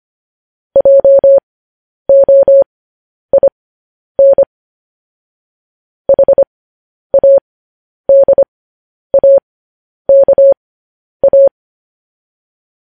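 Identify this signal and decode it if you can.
Morse code, heard as a keyed tone, about 25 wpm, spaced as for 10 wpm, 550 Hz. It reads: JOIN HADAKA
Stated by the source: Morse code news headlines practice transmission.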